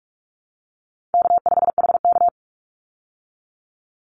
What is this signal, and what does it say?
Morse code, heard as a keyed tone, about 44 wpm, 700 Hz. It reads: K5HK
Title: Easy Call Signs Repeated 2x - 45wpm